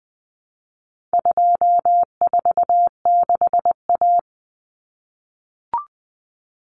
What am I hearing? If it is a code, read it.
Morse code, heard as a keyed tone, about 20 wpm, 700 Hz. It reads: 246A
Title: Common Cut Numbers - 4-Digit 20wpm